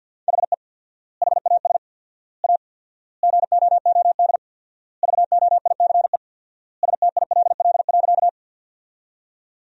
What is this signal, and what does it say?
Morse code, heard as a keyed tone, about 50 wpm, 700 Hz. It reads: HE HAS A GOOD VOICE STILL.
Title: 4x Word-Spacing - Sentences from Top 500 Words - 50wpm